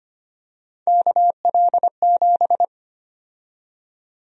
Morse code, heard as a keyed tone, about 25 wpm, 700 Hz. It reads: KL7
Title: USA Call Sign Prefixes - 25wpm